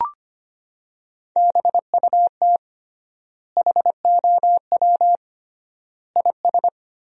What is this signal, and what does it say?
Morse code, heard as a keyed tone, about 25 wpm, 700 Hz. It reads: BUT HOW IS
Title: 3x Word-Spacing - Sentences from Top 100 Words - 25wpm